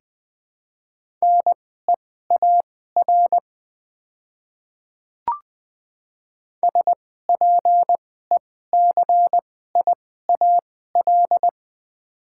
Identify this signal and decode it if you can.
Morse code, heard as a keyed tone, about 20 wpm, spaced as for 14 wpm, 700 Hz. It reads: NEAR SPECIAL
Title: 2x Character Spacing - Top 500 Words - 20wpm